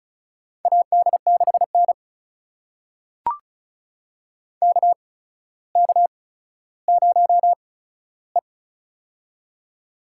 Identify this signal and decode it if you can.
Morse code, heard as a keyed tone, about 36 wpm, spaced as for 15 wpm, 700 Hz. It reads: AD6N KK0E
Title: USA Call Signs - 4 Characters - 8x Character Spacing - 35wpm